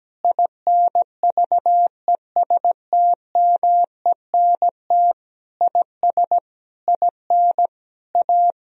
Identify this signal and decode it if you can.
Morse code, heard as a keyed tone, about 17 wpm, 700 Hz. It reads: INVESTMENT IS IN A
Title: Sentences from Top 1000 Words - Review Part 4 - 17wpm